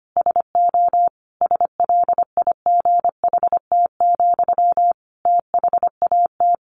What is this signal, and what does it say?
Morse code, heard as a keyed tone, about 25 wpm, 700 Hz. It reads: SO SLIGHT, THAT
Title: A Sound Of Thunder 25wpm